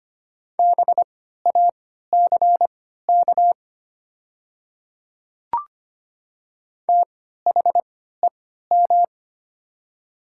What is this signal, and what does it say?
Morse code, heard as a keyed tone, about 25 wpm, spaced as for 14 wpm, 700 Hz. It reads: BACK THEM
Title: 3x Character Spacing - Top 200 Words - 25wpm